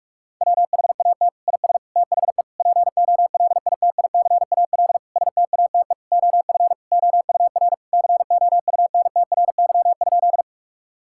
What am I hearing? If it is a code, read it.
Morse code, heard as a keyed tone, about 46 wpm, 700 Hz. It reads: WHAT IS THE POLITICAL STATE OF OUR COUNTRY?
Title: Sentences from Top 1000 Words - Review Part 5 - 45wpm